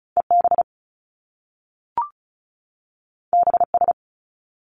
Morse code, heard as a keyed tone, about 36 wpm, 700 Hz. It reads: EB BS
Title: Top Two Letter Combinations in Words 35wpm